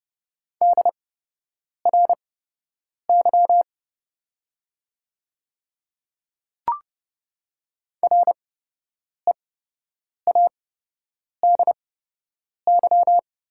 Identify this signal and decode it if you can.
Morse code, heard as a keyed tone, about 30 wpm, spaced as for 8 wpm, 700 Hz. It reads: DRY READY
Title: 8x Character Spacing - Top 500 Words - 30wpm